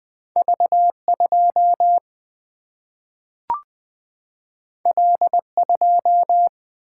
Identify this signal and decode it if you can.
Morse code, heard as a keyed tone, about 20 wpm, 700 Hz. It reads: V 2 L 2
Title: Single-Letter Single-Number 20wpm